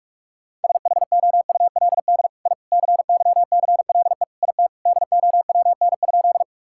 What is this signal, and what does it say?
Morse code, heard as a keyed tone, about 46 wpm, 700 Hz. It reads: SHOULD I CYCLE IT DOWN?
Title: Sentences from Top 1000 Words - Review Part 6 - 45wpm